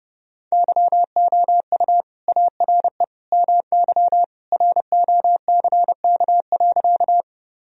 Morse code, heard as a keyed tone, about 30 wpm, 700 Hz. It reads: YOU ARE MY ROCK.
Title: Sentences from Top 1000 Words - Review Part 2 - 30wpm